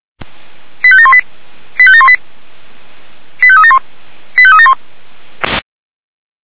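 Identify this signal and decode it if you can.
Five-tone selective call (ZVEI), heard as quick runs of five tones, tones of about 70 ms. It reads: 85718 85718 85371 85371
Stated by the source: ZVEI five-tone selective call received by SDR.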